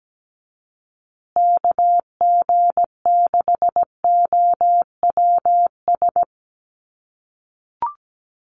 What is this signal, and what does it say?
Morse code, heard as a keyed tone, about 17 wpm, 700 Hz. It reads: KG6OWS
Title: USA Call Signs Repeated 3x 17wpm